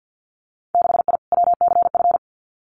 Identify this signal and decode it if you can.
Morse code, heard as a keyed tone, about 50 wpm, 700 Hz. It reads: 6I WCF
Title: Winter Field Day Exchanges Encore 50wpm